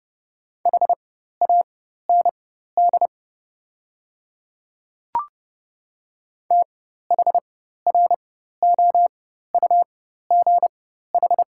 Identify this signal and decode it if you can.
Morse code, heard as a keyed tone, about 31 wpm, spaced as for 14 wpm, 690 Hz. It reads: HAND THROUGH
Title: 4x Character Spacing - Top 200 Words - 30wpm